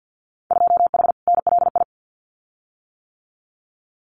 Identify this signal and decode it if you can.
Morse code, heard as a keyed tone, about 50 wpm, 700 Hz. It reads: 2H NLI
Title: Winter Field Day Exchanges 50wpm